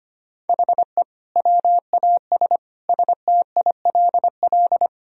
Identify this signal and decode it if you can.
Morse code, heard as a keyed tone, about 25 wpm, 700 Hz. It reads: HE WAS STILL
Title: Sentences from Top 1000 Words - Review Part 2 - 25wpm